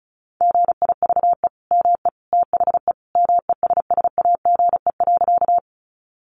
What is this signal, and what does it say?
Morse code, heard as a keyed tone, about 34 wpm, 700 Hz. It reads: GIVE ME THE MESSAGE.
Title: Sentences from Top 1100 Words 35wpm